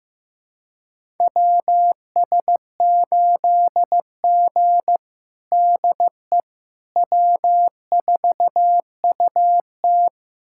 Morse code, heard as a keyed tone, about 15 wpm, 700 Hz. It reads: WS8G DE W4UT
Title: USA Callsign DE Callsign - 4 Characters - 15wpm